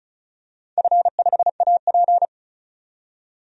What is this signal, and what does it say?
Morse code, heard as a keyed tone, about 34 wpm, 700 Hz. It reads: F5AP
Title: DXCC-100 Call Signs - 4 Characters - 35wpm